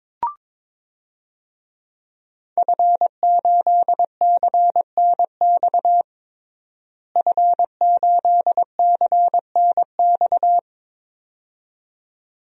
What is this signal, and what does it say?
Morse code, heard as a keyed tone, about 22 wpm, 700 Hz. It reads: F8CNX F8CNX
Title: DXCC-100 Call Signs - 5 Characters - Repeated 2x - 22wpm